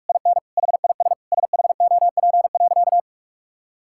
Morse code, heard as a keyed tone, about 42 wpm, 700 Hz. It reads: IN HIS SHOP.
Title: Sentences from Top 1600 Words 45wpm